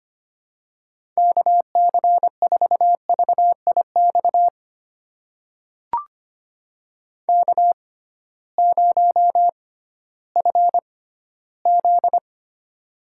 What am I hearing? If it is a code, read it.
Morse code, heard as a keyed tone, about 25 wpm, spaced as for 15 wpm, 700 Hz. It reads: KC4VIX K0FZ
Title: USA Call Signs - 6x Character Spacing - 25wpm